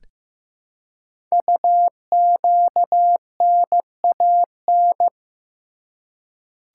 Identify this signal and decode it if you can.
Morse code, heard as a keyed tone, about 15 wpm, 700 Hz. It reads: UQNAN
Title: Code Groups - 5 Letters - 15wpm